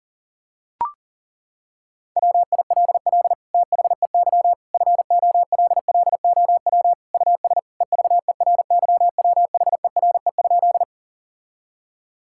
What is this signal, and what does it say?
Morse code, heard as a keyed tone, about 40 wpm, 700 Hz. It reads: WILL THEY FOLLOW US EVERYWHERE?
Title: Sentences from Top 1600 Words 40wpm